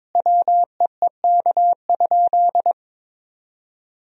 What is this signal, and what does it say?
Morse code, heard as a keyed tone, about 22 wpm, 700 Hz. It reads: WEEK?
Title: Sentences from Top 1000 Words - Review Part 2 - 22wpm